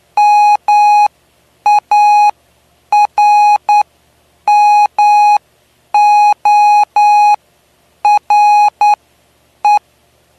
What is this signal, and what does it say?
Morse code, heard as a keyed tone, about 9 wpm, spaced as for 8 wpm, 830 Hz. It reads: MARMORE
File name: cw / py2lcd / 09-PY2LCD-Li��o 9